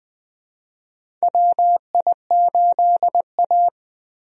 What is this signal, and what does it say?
Morse code, heard as a keyed tone, about 20 wpm, 700 Hz. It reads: WI8A
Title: USA Call Signs - 4 Characters - Repeated 3x - 20wpm